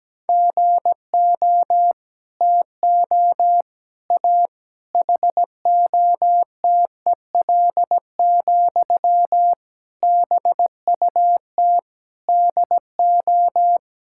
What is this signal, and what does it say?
Morse code, heard as a keyed tone, about 17 wpm, 700 Hz. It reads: GO TO A HOTEL, BUT DO
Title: Sentences from Top 1600 Words 17wpm